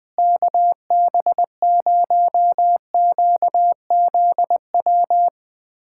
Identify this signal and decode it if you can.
Morse code, heard as a keyed tone, about 20 wpm, 700 Hz. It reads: KB0QZW
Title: USA Callsign DE Callsign 20wpm